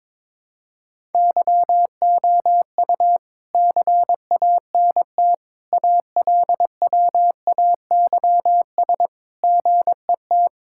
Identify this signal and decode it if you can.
Morse code, heard as a keyed tone, about 22 wpm, 700 Hz. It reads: YOU CANT ALWAYS GET
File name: Sentences with 7-Word Phrases - 22wpm